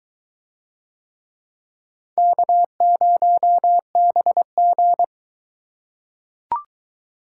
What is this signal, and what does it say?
Morse code, heard as a keyed tone, about 22 wpm, 700 Hz. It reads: K0BG